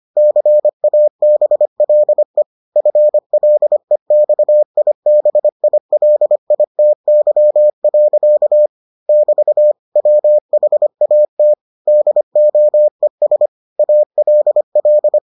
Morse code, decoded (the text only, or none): CABLE FLEXIBILITY. <BT> WHAT DOES ALL